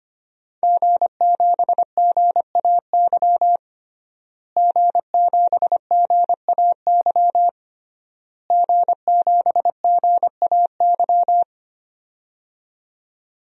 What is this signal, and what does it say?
Morse code, heard as a keyed tone, about 25 wpm, 700 Hz. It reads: G7GAY G7GAY G7GAY
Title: DXCC-100 Call Signs - 5 Characters - Repeated 3x - 25wpm